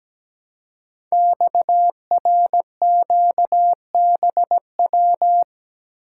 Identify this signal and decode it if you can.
Morse code, heard as a keyed tone, about 17 wpm, 700 Hz. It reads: XRQBW
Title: Code Groups - 5 Letters - 17wpm